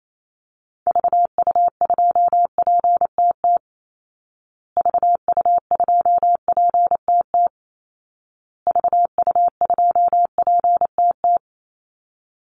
Morse code, heard as a keyed tone, about 28 wpm, 700 Hz. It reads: VU2PTT VU2PTT VU2PTT
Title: Call Signs Repeated 2x - 28wpm